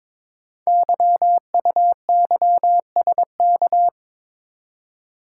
Morse code, heard as a keyed tone, about 22 wpm, 700 Hz. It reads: YUYSK
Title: Code Groups - 5 Letters - 22wpm